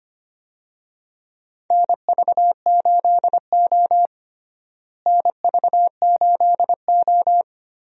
Morse code, heard as a keyed tone, about 25 wpm, 700 Hz. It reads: NV8O NV8O
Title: USA Call Signs - 4 Characters - Repeated 3x - 25wpm